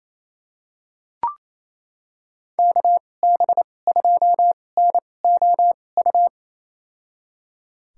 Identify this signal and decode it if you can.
Morse code, heard as a keyed tone, about 28 wpm, spaced as for 20 wpm, 700 Hz. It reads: KB2NOU
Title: USA Call Signs - 6 Characters - 2x Character Spacing - 28wpm